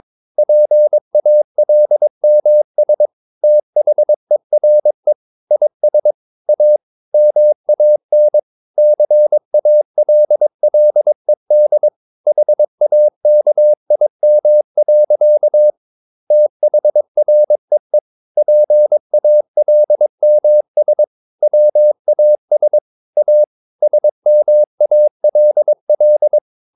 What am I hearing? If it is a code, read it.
Morse code, heard as a keyed tone, about 22 wpm, 600 Hz. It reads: PALMS THERE IS A MAN CALLED HAKIM. THREE PALMS WAS A SMALL
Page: Romanen Never af Ken Follet på engelsk.